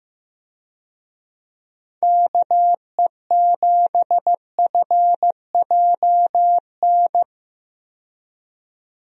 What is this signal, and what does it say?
Morse code, heard as a keyed tone, about 15 wpm, 700 Hz. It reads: KE7FJN
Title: USA Call Signs - 6 Characters - 15wpm